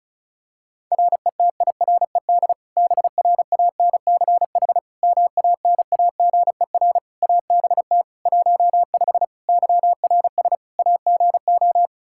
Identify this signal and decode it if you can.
Morse code, heard as a keyed tone, about 34 wpm, 700 Hz. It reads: RETIRED BRANCH MANAGER ABT 15 YRS AGO
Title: Practice QSO - Exchange 3 - More Info - 35wpm